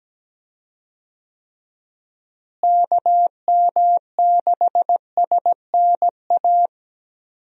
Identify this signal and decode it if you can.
Morse code, heard as a keyed tone, about 17 wpm, 700 Hz. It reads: KM6SNA
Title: USA Call Signs - 6 Characters - Repeated 3x - 17wpm